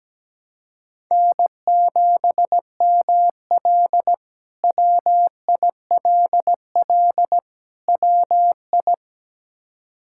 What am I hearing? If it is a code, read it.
Morse code, heard as a keyed tone, about 17 wpm, 700 Hz. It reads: N7ML WILL WI
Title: K1USN SST QSO Practice - 17wpm